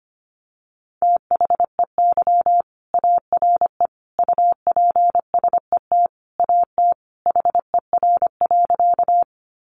Morse code, heard as a keyed tone, about 25 wpm, 700 Hz. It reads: THEY ARE UPSET AT HER.
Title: Sentences from Top 800 Words 25wpm